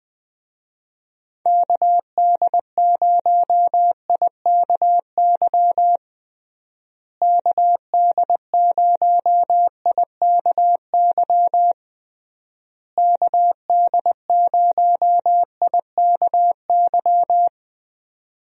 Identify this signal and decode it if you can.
Morse code, heard as a keyed tone, about 20 wpm, 700 Hz. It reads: KD0IKY KD0IKY KD0IKY
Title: USA Call Signs - 6 Characters - Repeated 3x - 20wpm